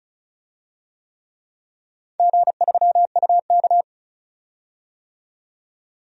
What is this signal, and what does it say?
Morse code, heard as a keyed tone, about 36 wpm, 700 Hz. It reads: G3UK
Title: DXCC-100 Call Signs - 4 Characters - 35wpm